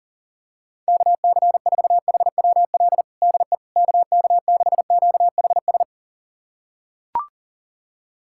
Morse code, heard as a keyed tone, about 40 wpm, 700 Hz. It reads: KC4HWL DE KK6QHS
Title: USA Callsign DE Callsign 40wpm